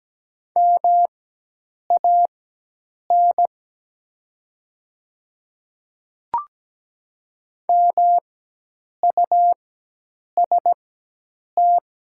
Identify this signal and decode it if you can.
Morse code, heard as a keyed tone, about 17 wpm, spaced as for 8 wpm, 700 Hz. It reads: MAN MUST
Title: 4x Character Spacing - Top 200 Words - 17wpm